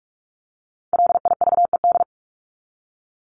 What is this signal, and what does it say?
Morse code, heard as a keyed tone, about 44 wpm, 700 Hz. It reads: LIVED
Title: ED Suffix - Top 100 Words with Base Words 45wpm